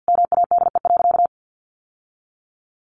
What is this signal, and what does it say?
Morse code, heard as a keyed tone, about 50 wpm, 700 Hz. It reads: MADE.
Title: Sentences from Top 1200 Words 50wpm